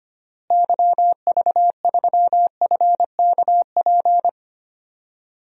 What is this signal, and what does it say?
Morse code, heard as a keyed tone, about 25 wpm, 700 Hz. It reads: YV3FKP
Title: DXCC-100 Call Signs - 6 Characters - Repeated 2x - 25wpm